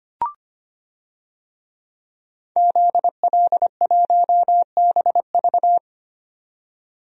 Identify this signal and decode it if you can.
Morse code, heard as a keyed tone, about 25 wpm, 700 Hz. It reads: ZL1BV
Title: DXCC-100 Call Signs - 5 Characters - 25wpm